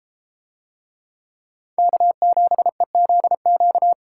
Morse code, heard as a keyed tone, about 32 wpm, 700 Hz. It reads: K7EZQ